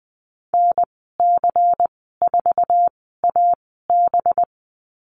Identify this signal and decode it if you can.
Morse code, heard as a keyed tone, about 20 wpm, spaced as for 15 wpm, 700 Hz. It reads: NC4AB
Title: 219 Lesson 35g - Number 0 - Call Signs 20wpm